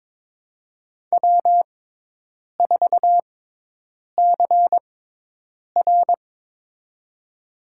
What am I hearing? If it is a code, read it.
Morse code, heard as a keyed tone, about 22 wpm, spaced as for 8 wpm, 700 Hz. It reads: W4CR
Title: USA Call Signs - 4 Characters - 6x Character Spacing - 22wpm